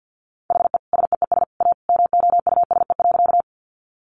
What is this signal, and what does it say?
Morse code, heard as a keyed tone, about 50 wpm, 700 Hz. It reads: HE SEES A MOUSE.
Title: Easy Phrases 50wpm